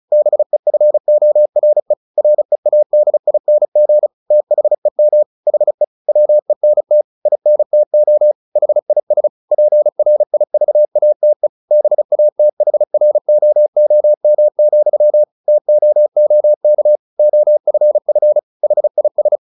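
Morse code, heard as a keyed tone, about 34 wpm, 600 Hz. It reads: BEFORE READING THEM HE WENT INTO HIS PRIVATE BATHROOM, TOOK OFF HIS
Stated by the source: Morse practice at 35 wpm from CW med Gnister, sending Ken Follett's novel Never in English.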